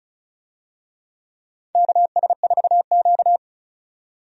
Morse code, heard as a keyed tone, about 36 wpm, 700 Hz. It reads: KS4Q
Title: USA Call Signs - 4 Characters - 35wpm